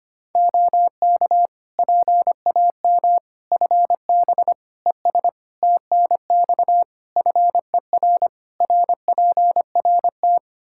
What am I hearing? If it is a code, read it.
Morse code, heard as a keyed tone, about 25 wpm, 700 Hz. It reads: OK PAM FB ES TNX FER RPRT
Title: K1USN SST QSO Practice 25wpm